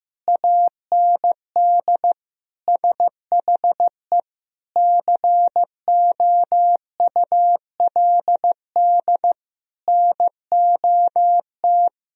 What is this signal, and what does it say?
Morse code, heard as a keyed tone, about 15 wpm, 700 Hz. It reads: AND SHE COULD NOT
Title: Sentences from Top 1000 Words - Review Part 3 - 15wpm